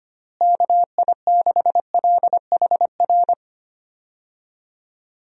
Morse code, heard as a keyed tone, about 25 wpm, 700 Hz. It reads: KI6LHR
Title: USA Call Signs - 6 Characters - Repeated 3x - 25wpm